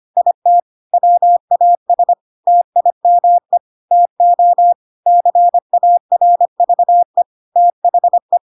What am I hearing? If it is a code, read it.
Morse code, heard as a keyed tone, about 25 wpm, 700 Hz. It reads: IT WAS TIME TO CARVE THE
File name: Pinocchio / Pinocchio ch3 25 25 ik1hge